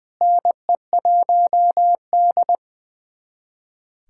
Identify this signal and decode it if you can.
Morse code, heard as a keyed tone, about 20 wpm, 700 Hz. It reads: NE1D
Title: USA Callsign DE Callsign - 4 Characters - 20wpm